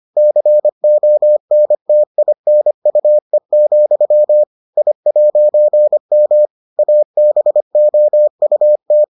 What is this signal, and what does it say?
Morse code, heard as a keyed tone, about 25 wpm, 600 Hz. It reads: CONTINUE, I'M ABOUT